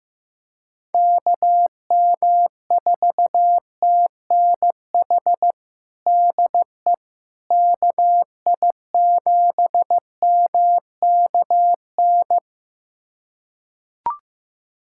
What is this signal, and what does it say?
Morse code, heard as a keyed tone, about 15 wpm, 700 Hz. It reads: KM4TNH DE KI7MKN
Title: USA Callsign DE Callsign - 6 Characters - 15wpm